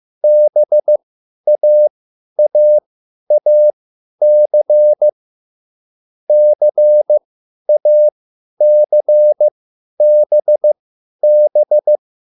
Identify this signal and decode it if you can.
Morse code, heard as a keyed tone, about 15 wpm, spaced as for 10 wpm, 600 Hz. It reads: BAAAC CACBB